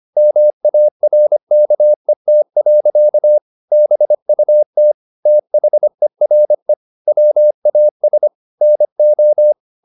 Morse code, heard as a keyed tone, about 25 wpm, 600 Hz. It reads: MARKET. BUT THERE WAS NO